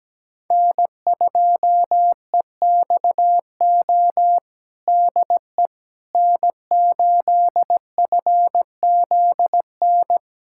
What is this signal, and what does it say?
Morse code, heard as a keyed tone, about 17 wpm, 700 Hz. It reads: N2EXO DE N8FZN
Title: USA Callsign DE Callsign - 5 Characters - 17wpm